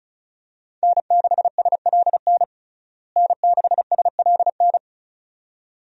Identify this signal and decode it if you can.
Morse code, heard as a keyed tone, about 36 wpm, 700 Hz. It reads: N6SLN N6SLN
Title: USA Call Signs - 5 Characters - Repeated 2x - 35wpm